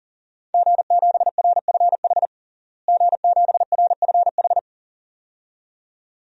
Morse code, heard as a keyed tone, about 40 wpm, 690 Hz. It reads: G7RFH G7RFH
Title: DXCC-100 Call Signs - 5 Characters - Repeated 2x - 40wpm